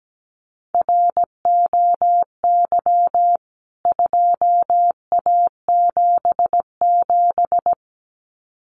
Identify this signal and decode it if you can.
Morse code, heard as a keyed tone, about 17 wpm, 700 Hz. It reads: ROY 2A77
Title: CWT Exchange 17wpm